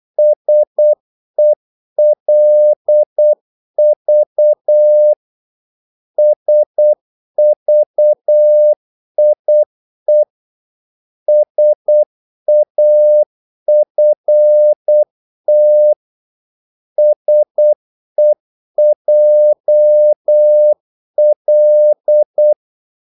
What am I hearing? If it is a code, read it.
Morse code, heard as a keyed tone, about 8 wpm, 600 Hz. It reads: SELV SVIE SAFT SEJL
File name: Korte ord DK 8wpm